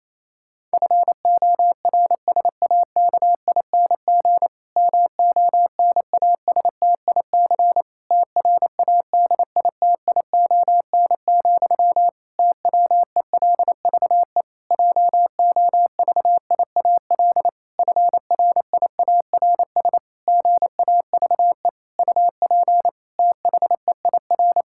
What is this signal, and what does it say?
Morse code, heard as a keyed tone, about 28 wpm, 700 Hz. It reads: FORSAKING MONASTIC TRADITION, TWELVE JOVIAL FRIARS GAVE UP THEIR
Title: Pangram Challenge v2 - 28wpm